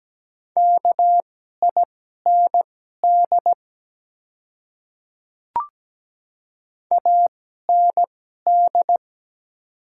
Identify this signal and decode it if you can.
Morse code, heard as a keyed tone, about 17 wpm, spaced as for 12 wpm, 700 Hz. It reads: KIND AND
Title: 2x Character Spacing - Top 300 Words - 17wpm